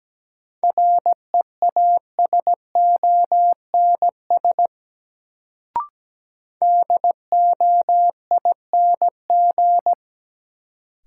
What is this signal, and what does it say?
Morse code, heard as a keyed tone, about 17 wpm, 700 Hz. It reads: REASONS DOING
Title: Sets of 4 Words - New Word Drill - 17wpm